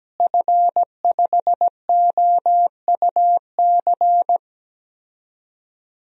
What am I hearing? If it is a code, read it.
Morse code, heard as a keyed tone, about 17 wpm, 700 Hz. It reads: F5OUC
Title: DXCC-100 Call Signs - 5 Characters - Repeated 2x - 17wpm